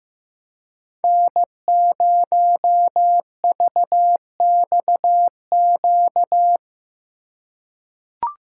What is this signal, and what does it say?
Morse code, heard as a keyed tone, about 15 wpm, 700 Hz. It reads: N0VXQ
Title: USA Call Signs - 5 Characters - 4x Character Spacing - 15wpm